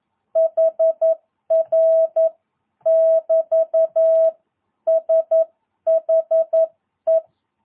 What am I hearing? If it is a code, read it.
Morse code, heard as a keyed tone, about 11 wpm, 650 Hz. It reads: HR <BT> SHE
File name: jh8cbh / 2022cw